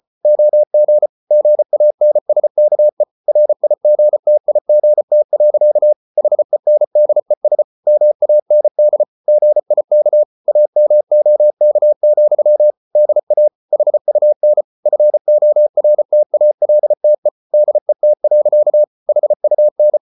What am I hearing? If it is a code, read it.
Morse code, heard as a keyed tone, about 34 wpm, 600 Hz. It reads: OG GANSKE RIGTIGT. HENDES MAND GIK AMOK, DA HUN FORTALTE DET. HUN
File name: Ferie QRQ 34wpm